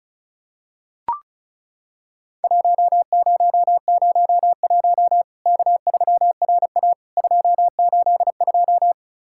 Morse code, heard as a keyed tone, about 36 wpm, 700 Hz. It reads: 1001 K3RA 282
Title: Worked All Europe DX Contest - QTC Exchange - 35wpm